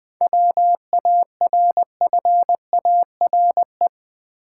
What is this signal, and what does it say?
Morse code, heard as a keyed tone, about 20 wpm, 700 Hz. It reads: WARFARE
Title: 7 Letter Words - First 3 Letters Given - 20wpm